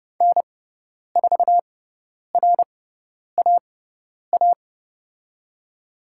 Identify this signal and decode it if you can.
Morse code, heard as a keyed tone, about 30 wpm, spaced as for 10 wpm, 700 Hz. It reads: N4RAA
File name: Lesson 16g - Letter c - Call Signs 30wpm